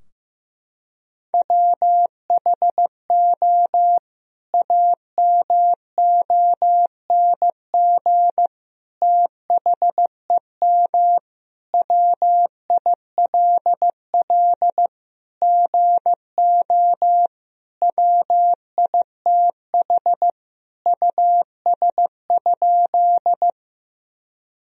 Morse code, heard as a keyed tone, about 15 wpm, 700 Hz. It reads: WHO AMONG THEM WILL GO WITH US?
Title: Sentences from Top 1000 Words - Review Part 3 - 15wpm